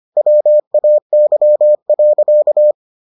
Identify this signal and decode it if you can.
Morse code, heard as a keyed tone, about 25 wpm, 600 Hz. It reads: WAY.